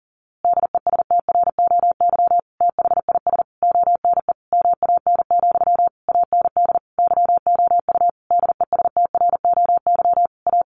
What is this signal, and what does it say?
Morse code, heard as a keyed tone, about 40 wpm, 700 Hz. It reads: DESTROY THIS ONE MAN, AND YOU DESTROY A
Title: A Sound Of Thunder 40wpm